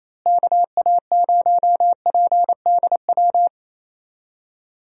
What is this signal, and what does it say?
Morse code, heard as a keyed tone, about 27 wpm, 700 Hz. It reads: KA0PDW
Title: USA Call Signs - 6 Characters - Repeated 3x - 28wpm